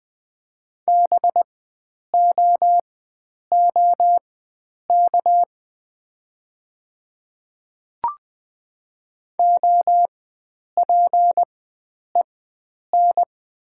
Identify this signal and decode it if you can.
Morse code, heard as a keyed tone, about 20 wpm, spaced as for 9 wpm, 700 Hz. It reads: BOOK OPEN
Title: 4x Character Spacing - Top 300 Words - 20wpm